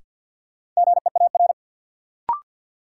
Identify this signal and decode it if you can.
Morse code, heard as a keyed tone, about 50 wpm, 700 Hz. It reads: GEAR
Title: Sentences from Top 1400 Words - New Word Drill Alt - 50wpm